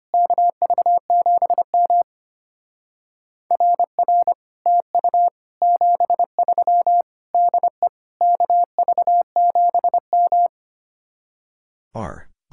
Morse code, heard as a keyed tone, about 31 wpm, 700 Hz. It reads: KV7M RR TU 73 DE KV7M
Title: Speed-Racing Common QSO Phrases - 25wpm